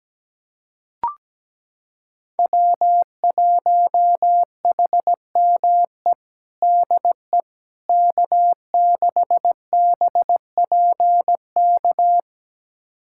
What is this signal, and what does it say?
Morse code, heard as a keyed tone, about 17 wpm, 700 Hz. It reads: W1HME DE K6BPK
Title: USA Callsign DE Callsign - 5 Characters - 17wpm